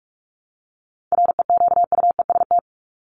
Morse code, heard as a keyed tone, about 44 wpm, 700 Hz. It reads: REQUEST
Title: Top 1751-2000 Words 45wpm